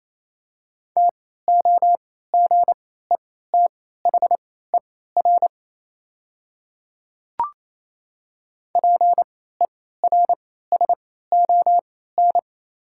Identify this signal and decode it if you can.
Morse code, heard as a keyed tone, about 28 wpm, spaced as for 16 wpm, 700 Hz. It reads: TOGETHER PERSON
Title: 3x Character Spacing - Top 500 Words - 28wpm